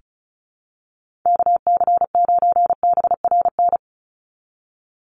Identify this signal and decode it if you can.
Morse code, heard as a keyed tone, about 34 wpm, 700 Hz. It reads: KC9BRN
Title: USA Call Signs 35wpm